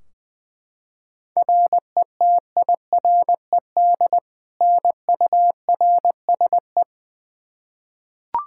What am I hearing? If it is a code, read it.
Morse code, heard as a keyed tone, about 20 wpm, 700 Hz. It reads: RETIRED NURSE
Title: K1USN SST QSO Practice 20wpm